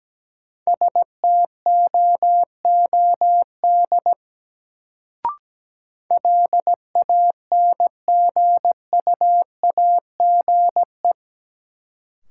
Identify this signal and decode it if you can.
Morse code, heard as a keyed tone, about 17 wpm, 700 Hz. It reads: STOOD LANGUAGE
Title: Sentences from Top 500 Words - New Word Drill - 17wpm